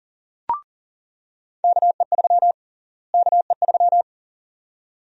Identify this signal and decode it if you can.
Morse code, heard as a keyed tone, about 37 wpm, 700 Hz. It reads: KE3 KE3
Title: USA Call Sign Prefixes - Repeated 2x - 40wpm